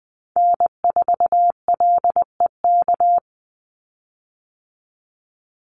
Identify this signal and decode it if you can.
Morse code, heard as a keyed tone, about 20 wpm, 700 Hz. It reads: N4LEK
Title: USA Call Signs 20wpm